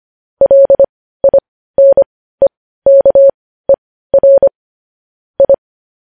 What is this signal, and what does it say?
Morse code, heard as a keyed tone, about 25 wpm, spaced as for 15 wpm, 550 Hz. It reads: LINEKER I
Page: News headlines in morse code at 15 wpm